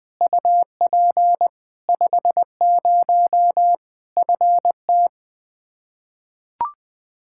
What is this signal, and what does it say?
Morse code, heard as a keyed tone, about 20 wpm, 700 Hz. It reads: UP 50 FT
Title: Practice QSO - Exchange 2 - Info - 20wpm